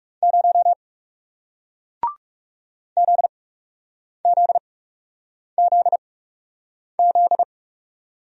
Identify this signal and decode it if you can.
Morse code, heard as a keyed tone, about 39 wpm, spaced as for 8 wpm, 700 Hz. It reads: 0 ZZZZ